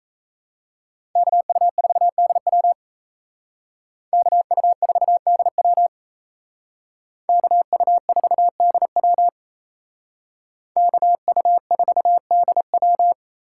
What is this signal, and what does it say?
Morse code, heard as a keyed tone, about 33 wpm, 700 Hz. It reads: KU4DW KU4DW KU4DW KU4DW
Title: Speed Racing - USA 2x2 Call Signs 28wpm